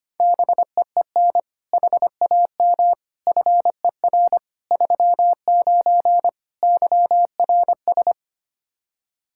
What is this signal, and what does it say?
Morse code, heard as a keyed tone, about 25 wpm, 700 Hz. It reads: BEEN HAM FER 39 YRS
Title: Practice QSO - Exchange 3 - More Info - 25wpm